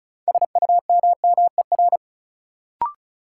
Morse code, heard as a keyed tone, about 36 wpm, 700 Hz. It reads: SUMMER
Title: Sentences from Top 1100 Words - New Word Drill Alt - 35wpm